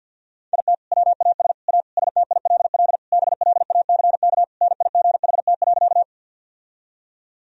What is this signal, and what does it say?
Morse code, heard as a keyed tone, about 46 wpm, 700 Hz. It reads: IT WAS A STILL BLACK NIGHT.
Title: Sentences from Top 1000 Words - Review Part 2 - 50wpm